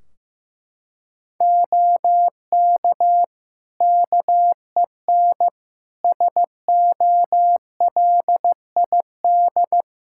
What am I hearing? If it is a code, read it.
Morse code, heard as a keyed tone, about 15 wpm, 700 Hz. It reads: OK KEN SOLID
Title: Practice QSO - Exchange 3 - More Info - 15wpm